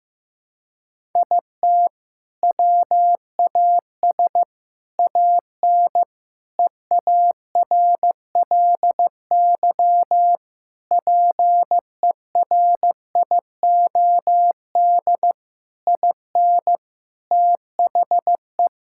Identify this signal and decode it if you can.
Morse code, heard as a keyed tone, about 15 wpm, 700 Hz. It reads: IT WAS AN EARLY PERIOD IN THE
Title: Sentences from Top 1000 Words - Review Part 5 - 15wpm